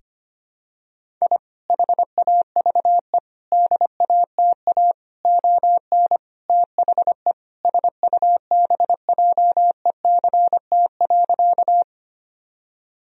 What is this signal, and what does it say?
Morse code, heard as a keyed tone, about 25 wpm, 700 Hz. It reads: I HAVE DATA ON THE SUBJECT.
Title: Sentences from Top 1000 Words - Review Part 3 - 25wpm